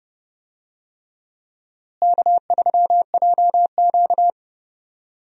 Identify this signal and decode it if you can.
Morse code, heard as a keyed tone, about 30 wpm, 700 Hz. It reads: K3JQ
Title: USA Call Signs - 4 Characters - Repeated 3x - 30wpm